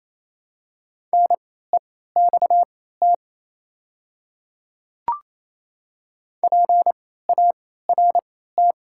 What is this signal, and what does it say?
Morse code, heard as a keyed tone, about 27 wpm, spaced as for 16 wpm, 700 Hz. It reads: NEXT PART
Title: 3x Character Spacing - Top 300 Words - 28wpm